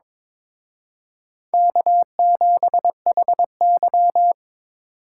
Morse code, heard as a keyed tone, about 26 wpm, 700 Hz. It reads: K7HY
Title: Speed Racing - USA 1x2 Call Signs 22wpm